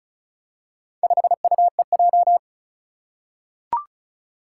Morse code, heard as a keyed tone, about 36 wpm, 700 Hz. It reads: 5UEJ
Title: Four Challenging Letter-Number Combinations 35wpm